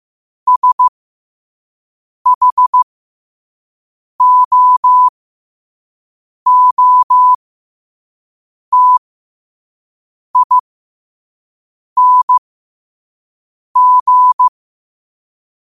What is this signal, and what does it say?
Morse code, heard as a keyed tone, about 15 wpm, spaced as for 5 wpm, 1000 Hz. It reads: SHOOTING
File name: qotdfiles / 5wpm